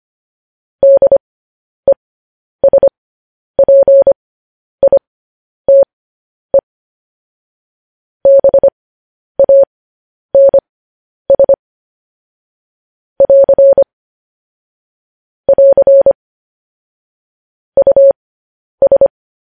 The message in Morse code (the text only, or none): DESPITE BANS <AR> <AR> US